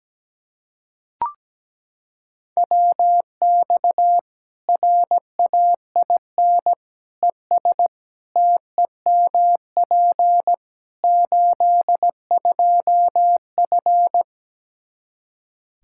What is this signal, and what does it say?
Morse code, heard as a keyed tone, about 17 wpm, 700 Hz. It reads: WX RAIN ES TEMP 82F
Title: Practice QSO - Exchange 2 - Info - 17wpm